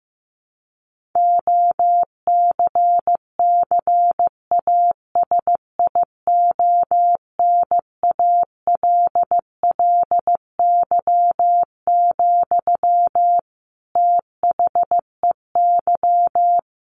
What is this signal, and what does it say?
Morse code, heard as a keyed tone, about 14 wpm, 700 Hz. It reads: OCCASIONALLY, THEY
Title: Sentences from Top 1400 Words 15wpm